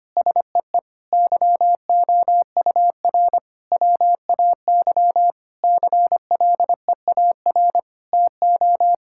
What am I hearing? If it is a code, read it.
Morse code, heard as a keyed tone, about 25 wpm, 700 Hz. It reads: SEE YOUR WAY CLEAR TO
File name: practice / 25wpm / Sentences with 9-Word Phrases - 25wpm